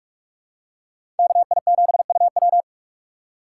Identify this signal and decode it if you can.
Morse code, heard as a keyed tone, about 47 wpm, 700 Hz. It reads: KI7UW
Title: USA Call Signs - 2x Character Spacing - 45wpm